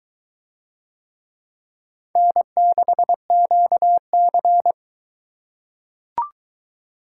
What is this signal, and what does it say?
Morse code, heard as a keyed tone, about 20 wpm, 700 Hz. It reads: N6QC